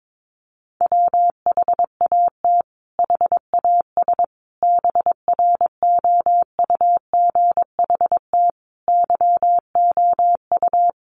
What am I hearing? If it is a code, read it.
Morse code, heard as a keyed tone, about 22 wpm, 700 Hz. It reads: WHAT HAS BROUGHT YOU
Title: Sentences from Top 500 Words 22wpm